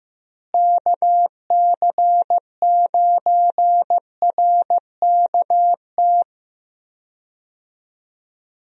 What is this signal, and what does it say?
Morse code, heard as a keyed tone, about 15 wpm, 700 Hz. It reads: KC9RKT
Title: USA Call Signs - 6 Characters - Repeated 2x - 15wpm